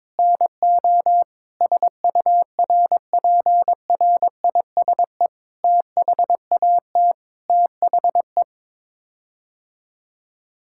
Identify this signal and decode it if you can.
Morse code, heard as a keyed tone, about 22 wpm, 700 Hz. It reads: NO SURPRISE THAT THE
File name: practice / 22wpm / Sets of 8 Words - 22wpm